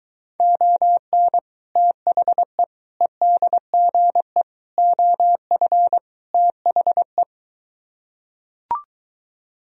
Morse code, heard as a keyed tone, about 23 wpm, 700 Hz. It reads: ON THE EDGE OF THE